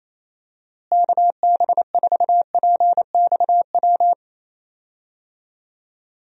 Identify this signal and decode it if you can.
Morse code, heard as a keyed tone, about 28 wpm, 700 Hz. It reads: KB4PXW
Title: USA Call Signs - 6 Characters - Repeated 2x - 28wpm